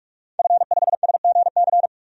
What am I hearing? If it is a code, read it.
Morse code, heard as a keyed tone, about 44 wpm, 690 Hz. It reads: F5SGC
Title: DXCC-100 Call Signs - 5 Characters - 45wpm